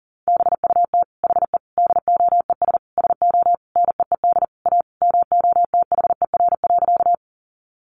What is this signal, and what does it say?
Morse code, heard as a keyed tone, about 40 wpm, 700 Hz. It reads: BUT HE DOES SO NEED A MOTHER.
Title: Sentences from Top 200 Words 40wpm